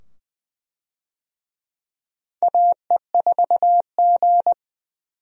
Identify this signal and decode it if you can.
Morse code, heard as a keyed tone, about 20 wpm, 700 Hz. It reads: AE4G
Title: USA Call Signs - 4 Characters - 20wpm